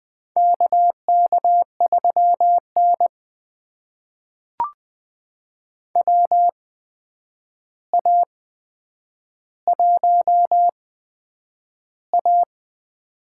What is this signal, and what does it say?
Morse code, heard as a keyed tone, about 20 wpm, spaced as for 9 wpm, 700 Hz. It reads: KK3N WA1A
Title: USA Call Signs - 4 Characters - 8x Character Spacing - 20wpm